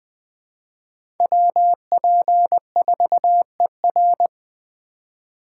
Morse code, heard as a keyed tone, about 21 wpm, 700 Hz. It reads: WP4ER
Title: Speed Racing - USA 2x2 Call Signs 17wpm